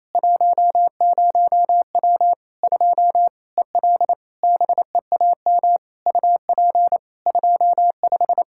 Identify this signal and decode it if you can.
Morse code, heard as a keyed tone, about 28 wpm, 700 Hz. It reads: 10W 2 EL BEAM UP 25
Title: Practice QSO - Exchange 2 - Info - 28wpm